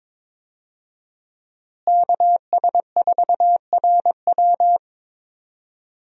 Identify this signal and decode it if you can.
Morse code, heard as a keyed tone, about 22 wpm, 700 Hz. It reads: KS4RW
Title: USA Call Signs - 5 Characters - Repeated 3x - 22wpm